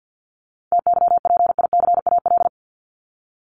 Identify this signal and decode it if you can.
Morse code, heard as a keyed tone, about 48 wpm, 700 Hz. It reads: TYPICAL
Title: Top 1001-1250 Words 50wpm